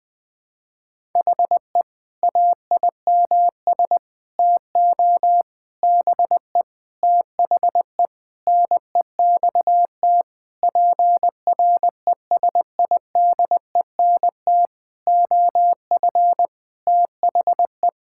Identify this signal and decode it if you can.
Morse code, heard as a keyed tone, about 20 wpm, 700 Hz. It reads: HE AIMS TO BE THE NEXT PRESIDENT OF THE
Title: Sentences with 8-Word Phrases - 20wpm